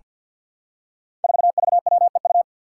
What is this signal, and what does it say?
Morse code, heard as a keyed tone, about 47 wpm, 700 Hz. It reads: 4VWEU
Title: Five Challenging Letter-Number Combinations 50wpm